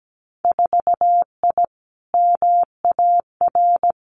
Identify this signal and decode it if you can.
Morse code, heard as a keyed tone, about 17 wpm, 700 Hz. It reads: 4I MAR
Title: Winter Field Day Exchanges Encore 17wpm